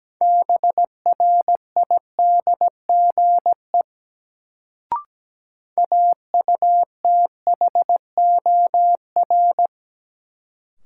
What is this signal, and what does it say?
Morse code, heard as a keyed tone, about 17 wpm, 700 Hz. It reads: BRIDGE AUTHOR
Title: Sentences from Top 1200 Words - New Word Drill - 17wpm